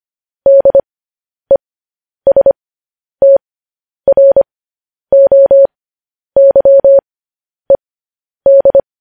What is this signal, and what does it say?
Morse code, heard as a keyed tone, about 25 wpm, spaced as for 10 wpm, 550 Hz. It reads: DESTROYED
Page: news headlines in morse code at 10 wpm